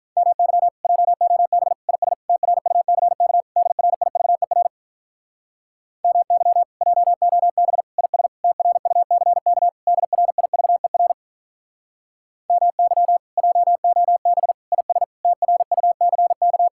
Morse code, heard as a keyed tone, about 47 wpm, 700 Hz. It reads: MY JOB IS TRUCK DRIVER MY JOB IS TRUCK DRIVER MY JOB IS TRUCK